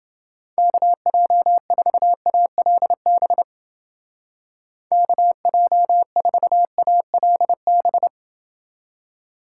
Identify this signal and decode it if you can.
Morse code, heard as a keyed tone, about 28 wpm, 700 Hz. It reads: KJ4ALB KJ4ALB